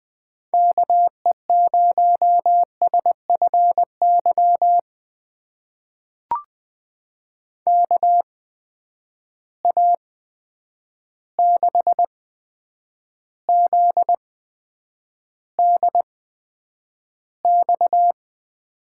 Morse code, heard as a keyed tone, about 20 wpm, spaced as for 9 wpm, 700 Hz. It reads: KE0SFY KA6ZDX
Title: USA Call Signs - 6 Characters - 8x Character Spacing - 20wpm